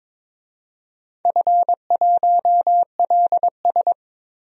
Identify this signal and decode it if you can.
Morse code, heard as a keyed tone, about 22 wpm, 700 Hz. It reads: F1LS
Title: DXCC-100 Call Signs - 4 Characters - 22wpm